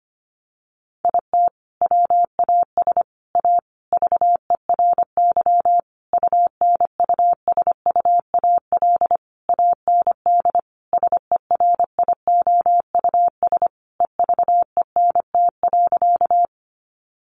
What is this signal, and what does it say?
Morse code, heard as a keyed tone, about 25 wpm, 700 Hz. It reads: IT WAS A VERY UNUSUAL AND SERIOUS EVENT.
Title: Sentences from Top 1400 Words 25wpm